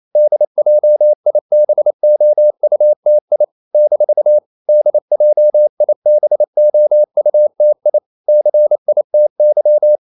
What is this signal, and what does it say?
Morse code, heard as a keyed tone, about 28 wpm, 600 Hz. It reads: DJIBOUTI - DJIBOUTI CITY
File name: CW / Practice / NationalCapitals28